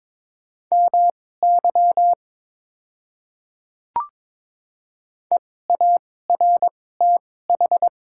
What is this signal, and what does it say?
Morse code, heard as a keyed tone, about 22 wpm, spaced as for 16 wpm, 700 Hz. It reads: MY EARTH
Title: 2x Character Spacing - Top 200 Words - 22wpm